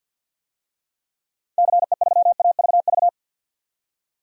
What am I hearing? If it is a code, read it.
Morse code, heard as a keyed tone, about 53 wpm, 700 Hz. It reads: CE3AVV
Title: DXCC-100 Call Signs - 6 Characters - 50wpm